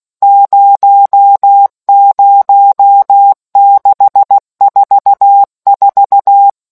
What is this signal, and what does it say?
Morse code, heard as a keyed tone, about 16 wpm, 800 Hz. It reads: 00644